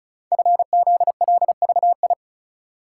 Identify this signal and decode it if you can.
Morse code, heard as a keyed tone, about 36 wpm, 700 Hz. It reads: FZLVI
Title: Code Groups - 5 Letters - 35wpm